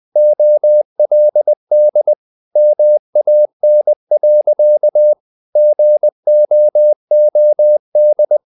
Morse code, heard as a keyed tone, about 20 wpm, 600 Hz. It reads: OLD MAN. GOOD
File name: the old man and the sea / the old man and the sea part 1 20wpm